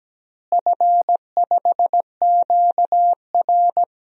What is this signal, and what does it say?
Morse code, heard as a keyed tone, about 17 wpm, 700 Hz. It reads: F5QR
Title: DXCC-100 Call Signs - 4 Characters - Repeated 2x - 17wpm